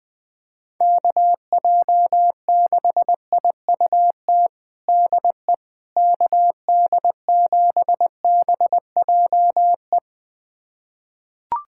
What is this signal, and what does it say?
Morse code, heard as a keyed tone, about 20 wpm, 700 Hz. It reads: KJ6IUT DE KD7BJE
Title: USA Callsign DE Callsign 20wpm